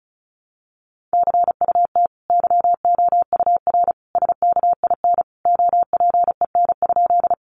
Morse code, heard as a keyed tone, about 34 wpm, 700 Hz. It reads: CUT YOUR SKIN OPEN?
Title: Sentences from Top 1300 Words 35wpm